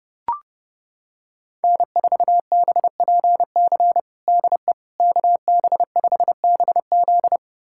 Morse code, heard as a keyed tone, about 30 wpm, 700 Hz. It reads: N4BPC DE KB5BZ
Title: USA Callsign DE Callsign - 5 Characters - 30wpm